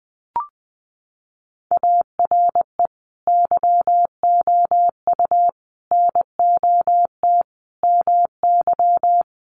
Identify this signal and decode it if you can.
Morse code, heard as a keyed tone, about 20 wpm, 700 Hz. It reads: ARE YOU NOT MY
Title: Sentences from Top 1300 Words 20wpm